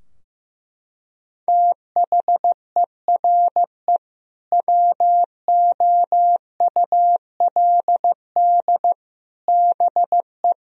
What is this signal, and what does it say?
Morse code, heard as a keyed tone, about 16 wpm, 700 Hz. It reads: THERE WOULD BE
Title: Sentences from Top 1000 Words - Review Part 4 - 15wpm